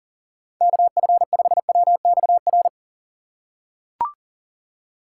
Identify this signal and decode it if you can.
Morse code, heard as a keyed tone, about 38 wpm, 700 Hz. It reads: KF5WXR
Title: USA Call Signs - 6 Characters - 3x Character Spacing - 40wpm